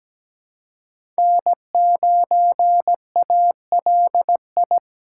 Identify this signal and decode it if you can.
Morse code, heard as a keyed tone, about 17 wpm, 700 Hz. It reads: N9ALI
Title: USA Call Signs - 5 Characters - 3x Character Spacing - 17wpm